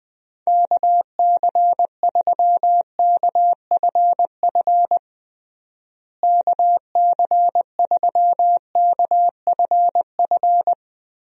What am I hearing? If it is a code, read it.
Morse code, heard as a keyed tone, about 20 wpm, 700 Hz. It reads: KC3KFF KC3KFF
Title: USA Call Signs - 6 Characters - Repeated 2x - 20wpm